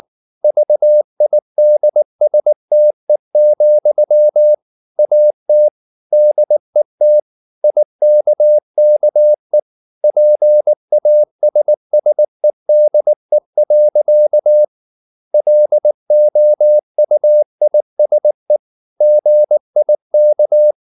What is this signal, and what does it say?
Morse code, heard as a keyed tone, about 19 wpm, 600 Hz. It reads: VIDSTE, AT DET IKKE PASSEDE. LOUISE GIK
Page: Ferie 19 wpm